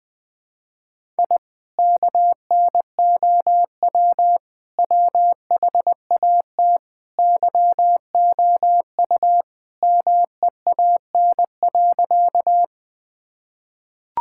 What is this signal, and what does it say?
Morse code, heard as a keyed tone, about 20 wpm, 700 Hz. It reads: I KNOW WHAT YOU MEAN.
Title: Sentences with 2-Word Phrases - Spoken Beginning - 20wpm